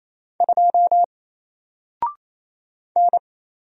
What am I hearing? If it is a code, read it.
Morse code, heard as a keyed tone, about 27 wpm, 700 Hz. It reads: 2 N
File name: Single-Letter-Number-28wpm